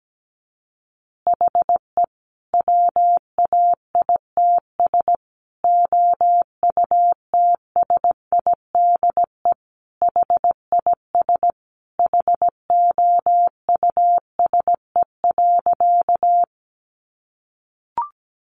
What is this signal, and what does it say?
Morse code, heard as a keyed tone, about 17 wpm, 700 Hz. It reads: HE WAITS OUTSIDE HIS HOUSE.
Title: Additional Easy Phrases 17wpm